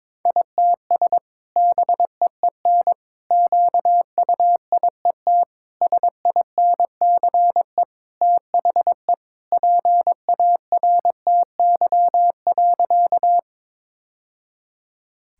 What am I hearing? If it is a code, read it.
Morse code, heard as a keyed tone, about 22 wpm, 700 Hz. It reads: ITS BEEN QUIET SINCE THE PARTY.
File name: practice / 22wpm / Sentences with 2-Word Phrases - 22wpm